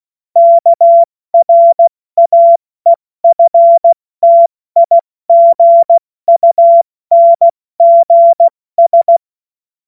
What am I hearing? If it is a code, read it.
Morse code, heard as a keyed tone, about 16 wpm, 680 Hz. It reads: KRAEFTIGUNGS
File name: Morsekurs / Witze / Witz 10 Juni Mai 2025 70 F80